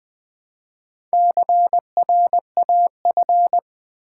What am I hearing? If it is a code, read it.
Morse code, heard as a keyed tone, about 20 wpm, 700 Hz. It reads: CRAF
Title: Build A Word - 1 Letter At A Time - 5-Letter Words - 20wpm